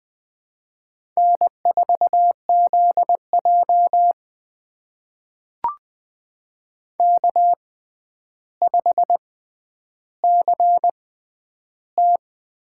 Morse code, heard as a keyed tone, about 20 wpm, spaced as for 10 wpm, 700 Hz. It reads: N4ZJ K5CT
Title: USA Call Signs - 4 Characters - 6x Character Spacing - 20wpm